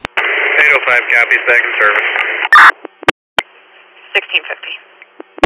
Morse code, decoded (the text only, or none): none